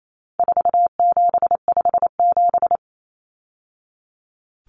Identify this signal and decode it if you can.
Morse code, heard as a keyed tone, about 28 wpm, 700 Hz. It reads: 4757
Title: Numbers - 4-Digit 28wpm